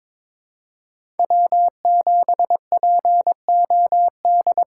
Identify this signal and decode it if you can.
Morse code, heard as a keyed tone, about 22 wpm, 700 Hz. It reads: W7POD
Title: USA Call Signs - 3x Character Spacing - 22wpm